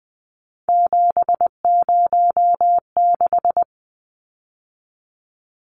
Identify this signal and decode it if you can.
Morse code, heard as a keyed tone, about 20 wpm, 700 Hz. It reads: 706
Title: Numbers - 3-Digit 20wpm